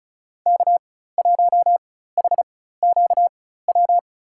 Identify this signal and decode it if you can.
Morse code, heard as a keyed tone, about 36 wpm, spaced as for 16 wpm, 700 Hz. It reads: K1HQW
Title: USA Call Signs - 5 Characters - 4x Character Spacing - 35wpm